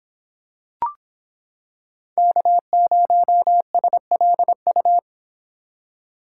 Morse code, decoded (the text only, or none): K0SLU